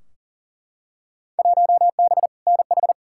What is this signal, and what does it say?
Morse code, heard as a keyed tone, about 40 wpm, 700 Hz. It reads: 1B NH
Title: Field Day Exchanges Encore - 40wpm